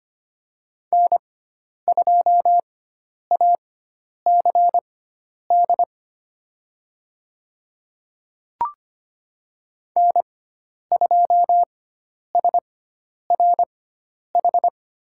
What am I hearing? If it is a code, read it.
Morse code, heard as a keyed tone, about 25 wpm, spaced as for 10 wpm, 700 Hz. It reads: N2ACD N2SRH
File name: Lesson 16g - Letter c - Call Signs 25wpm